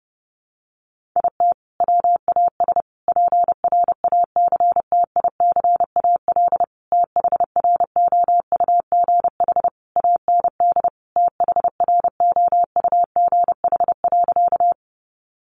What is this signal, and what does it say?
Morse code, heard as a keyed tone, about 30 wpm, 700 Hz. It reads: IT WAS PRACTICAL THROUGH AND THROUGH.
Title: Sentences from Top 1400 Words 30wpm